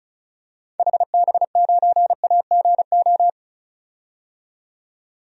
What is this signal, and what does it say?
Morse code, heard as a keyed tone, about 34 wpm, 700 Hz. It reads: HB9AGO
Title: DXCC-100 Call Signs - 6 Characters - 35wpm